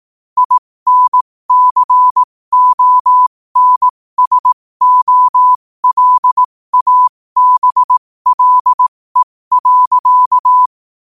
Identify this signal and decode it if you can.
Morse code, heard as a keyed tone, about 18 wpm, spaced as for 15 wpm, 1000 Hz. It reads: INCONSOLABLE.
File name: qotdfiles / 15wpm